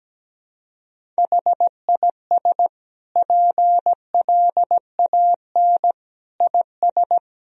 Morse code, heard as a keyed tone, about 17 wpm, 700 Hz. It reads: HIS PLAN IS
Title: Sentences from Top 500 Words and Abbreviations 17wpm